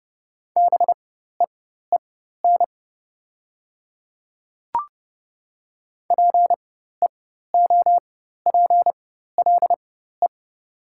Morse code, heard as a keyed tone, about 30 wpm, spaced as for 14 wpm, 700 Hz. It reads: BEEN PEOPLE
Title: 4x Character Spacing - Top 100 Words - 30wpm